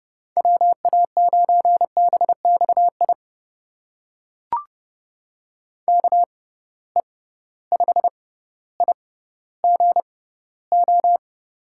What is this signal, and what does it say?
Morse code, heard as a keyed tone, about 30 wpm, spaced as for 15 wpm, 700 Hz. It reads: WA9BXI KE5IGO
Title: USA Call Signs - 6 Characters - 6x Character Spacing - 30wpm